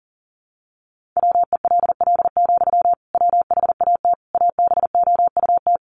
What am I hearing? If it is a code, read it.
Morse code, heard as a keyed tone, about 40 wpm, 700 Hz. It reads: WELL, WHAT ABOUT
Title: Sentences from Top 500 Words 40wpm